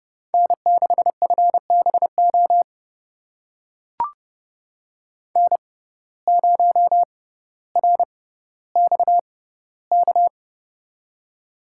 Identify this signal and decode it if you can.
Morse code, heard as a keyed tone, about 30 wpm, spaced as for 15 wpm, 700 Hz. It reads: N6FBO N0RXK
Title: USA Call Signs - 5 Characters - 6x Character Spacing - 30wpm